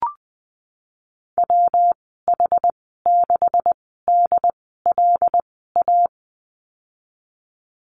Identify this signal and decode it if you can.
Morse code, heard as a keyed tone, about 20 wpm, spaced as for 15 wpm, 700 Hz. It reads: WH6DLA
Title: 167 Lesson 27g - Number 7 - Call Signs 20wpm